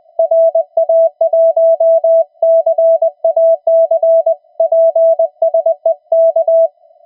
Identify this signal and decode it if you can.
Morse code, heard as a keyed tone, about 20 wpm, 660 Hz. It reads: RA1CAC PSE K
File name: attachments / TX 28060